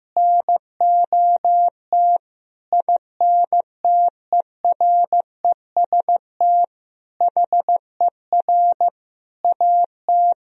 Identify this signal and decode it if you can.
Morse code, heard as a keyed tone, about 16 wpm, 700 Hz. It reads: NOT INTEREST HER AT
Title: Sentences from Top 1000 Words - Review Part 2 - 15wpm